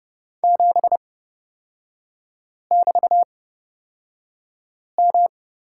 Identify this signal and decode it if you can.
Morse code, heard as a keyed tone, about 30 wpm, spaced as for 5 wpm, 700 Hz. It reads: 7<BT>M
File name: Lesson 36e - BT - Three Characters 30wpm